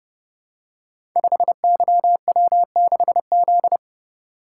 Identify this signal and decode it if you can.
Morse code, heard as a keyed tone, about 30 wpm, 700 Hz. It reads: 5YW6Z
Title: Code Groups - 5 Letter:Numbers - 30wpm